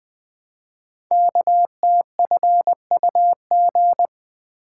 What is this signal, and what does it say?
Morse code, heard as a keyed tone, about 20 wpm, 700 Hz. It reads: KTFUG
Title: Code Groups - 5 Letters - 20wpm